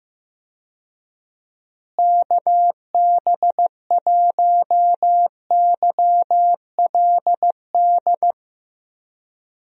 Morse code, heard as a keyed tone, about 15 wpm, 700 Hz. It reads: KB1YLD
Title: USA Call Signs - 6 Characters - Repeated 2x - 15wpm